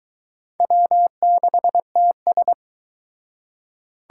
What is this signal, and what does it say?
Morse code, heard as a keyed tone, about 23 wpm, 700 Hz. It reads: W6TS